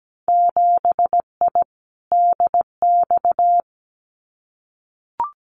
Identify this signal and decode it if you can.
Morse code, heard as a keyed tone, about 17 wpm, 700 Hz. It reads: 7I DX
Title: Winter Field Day Exchanges Encore 17wpm